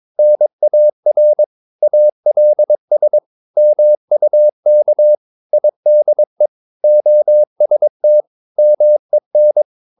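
Morse code, heard as a keyed tone, about 22 wpm, 600 Hz. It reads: NAR ALS MUK IDE OST MEN
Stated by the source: Morse practice of short Danish words at 22 wpm, from CW med Gnister.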